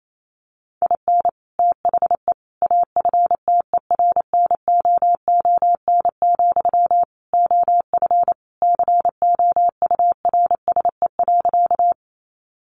Morse code, heard as a keyed tone, about 28 wpm, 700 Hz. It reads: IN THE AFTERNOON, OF COURSE.
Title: Sentences from Top 1500 Words 28wpm